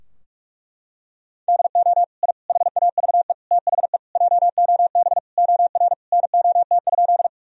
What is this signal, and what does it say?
Morse code, heard as a keyed tone, about 41 wpm, 700 Hz. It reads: DO I HAVE THE JOB OR NOT?
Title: Sentences from Top 1000 Words - Review Part 3 - 45wpm